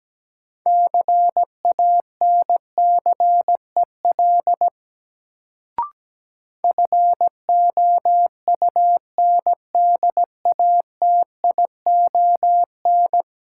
Morse code, heard as a keyed tone, about 17 wpm, 700 Hz. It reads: CANCEL FOUNDATION
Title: Sentences from Top 1200 Words - New Word Drill - 17wpm